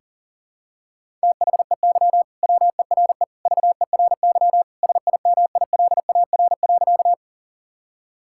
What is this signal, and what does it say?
Morse code, heard as a keyed tone, about 40 wpm, 700 Hz. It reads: THEY WERE VERY SIMILAR.
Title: Sentences from Top 1000 Words - Review Part 5 - 40wpm